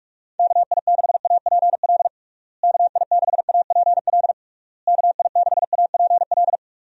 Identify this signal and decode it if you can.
Morse code, heard as a keyed tone, about 46 wpm, 700 Hz. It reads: KI6APL KI6APL KI6APL
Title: USA Call Signs - 6 Characters - Repeated 3x - 45wpm